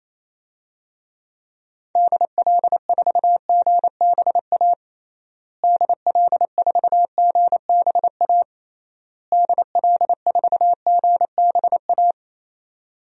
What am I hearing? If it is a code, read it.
Morse code, heard as a keyed tone, about 28 wpm, 700 Hz. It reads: DL4GBA DL4GBA DL4GBA
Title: DXCC-100 Call Signs - 6 Characters - Repeated 3x - 28wpm